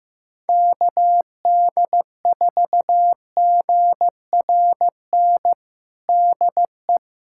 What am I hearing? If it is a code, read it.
Morse code, heard as a keyed tone, about 15 wpm, 700 Hz. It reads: KD4GRN DE
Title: USA Callsign DE Callsign - 6 Characters - 15wpm